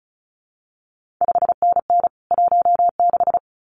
Morse code, heard as a keyed tone, about 36 wpm, 700 Hz. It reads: 5NN 16
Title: CQ WW DX Contest Exchange - 35wpm